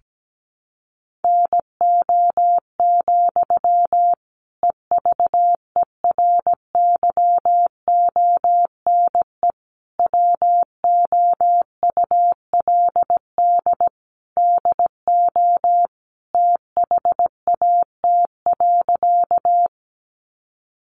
Morse code, heard as a keyed tone, about 17 wpm, 700 Hz. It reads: NO, EVERYONE WOULD DO THAT.
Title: Sentences from Top 800 Words 17wpm